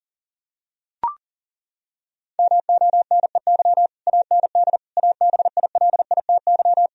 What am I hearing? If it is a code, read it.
Morse code, heard as a keyed tone, about 40 wpm, 700 Hz. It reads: MONEY AND ABILITY
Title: Sentences from Top 1000 Words - Review Part 3 - 40wpm